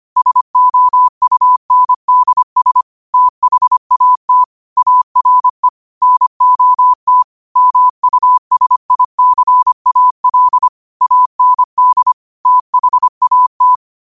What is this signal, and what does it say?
Morse code, heard as a keyed tone, about 25 wpm, 1000 Hz. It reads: SOUNDS THAT ARE NOT MUSICAL AND THAT
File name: qotdfiles / 25wpm